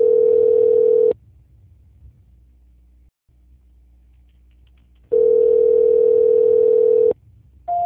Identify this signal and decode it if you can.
Ringback tone (North America), 440 + 480 Hz.
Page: Bangu - RJ